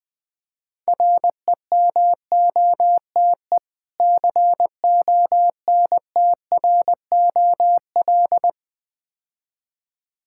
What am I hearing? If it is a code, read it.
Morse code, heard as a keyed tone, about 20 wpm, 700 Hz. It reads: REMOTE CONTROL
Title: Easy Words - Speed-Racing with Hints - 20wpm